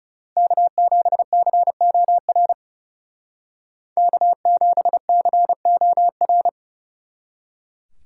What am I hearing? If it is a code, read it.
Morse code, heard as a keyed tone, about 39 wpm, 700 Hz. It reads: K7COR K7COR